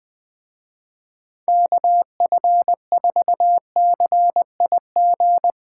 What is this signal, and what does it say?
Morse code, heard as a keyed tone, about 21 wpm, 700 Hz. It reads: KF4CIG
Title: Speed Racing - USA 2x3 Call Signs 20wpm